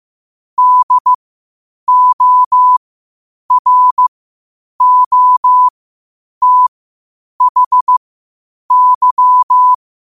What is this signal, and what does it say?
Morse code, heard as a keyed tone, about 15 wpm, spaced as for 8 wpm, 1000 Hz. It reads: DOROTHY
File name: qotdfiles / 8wpm